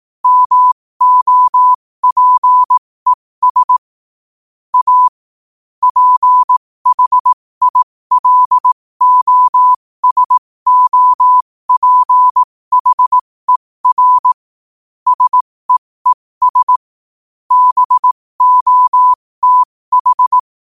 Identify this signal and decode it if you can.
Morse code, heard as a keyed tone, about 18 wpm, spaced as for 15 wpm, 1000 Hz. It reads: MOPES A PHILOSOPHER SEES BOTH
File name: qotdfiles / 15wpm